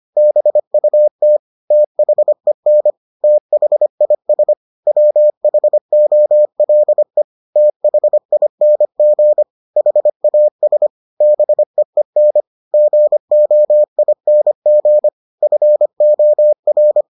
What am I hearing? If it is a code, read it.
Morse code, heard as a keyed tone, about 25 wpm, 600 Hz. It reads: BUT THEN THIS WHOLE THING HAS BEEN GOING FOR